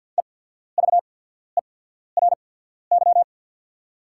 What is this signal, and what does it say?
Morse code, heard as a keyed tone, about 48 wpm, spaced as for 14 wpm, 690 Hz. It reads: EVERY
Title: 8x Character Spacing - Top 300 Words - 50wpm